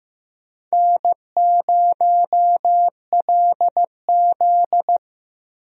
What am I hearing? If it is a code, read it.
Morse code, heard as a keyed tone, about 15 wpm, 700 Hz. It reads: N0LZ
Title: USA Call Signs - 4 Characters - Repeated 2x - 15wpm